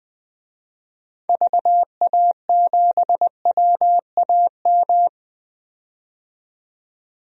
Morse code, heard as a keyed tone, about 20 wpm, 700 Hz. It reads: VA7WAM
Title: DXCC-100 Call Signs - 6 Characters - 20wpm